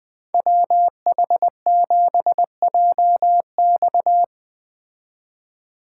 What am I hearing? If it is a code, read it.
Morse code, heard as a keyed tone, about 20 wpm, 700 Hz. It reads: WH7JX
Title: Speed Racing - USA 2x2 Call Signs 17wpm